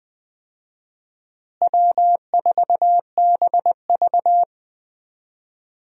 Morse code, heard as a keyed tone, about 20 wpm, 700 Hz. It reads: W4BV
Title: USA Call Signs - 4 Characters - Repeated 2x - 20wpm